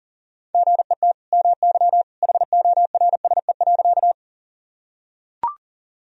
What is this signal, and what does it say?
Morse code, heard as a keyed tone, about 40 wpm, 700 Hz. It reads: GET MY HORSE.
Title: Sentences from Top 1000 Words - Review Part 1 - 40wpm